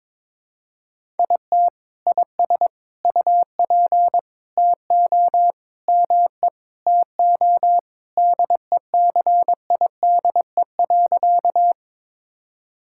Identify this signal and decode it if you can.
Morse code, heard as a keyed tone, about 22 wpm, 700 Hz. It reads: IT IS UP TO ME TO DECIDE.
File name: practice / 22wpm / Sentences with 2-Word Phrases - 22wpm